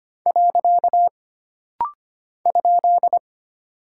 Morse code, heard as a keyed tone, about 25 wpm, 700 Hz. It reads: . ?
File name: Single Punctuation-Prosign - Rapid-Fire - 25wpm